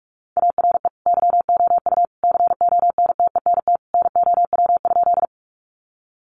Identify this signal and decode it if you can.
Morse code, heard as a keyed tone, about 44 wpm, 700 Hz. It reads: ARE YOU CONTENT NOW?
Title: Sentences from Top 900 Words 45wpm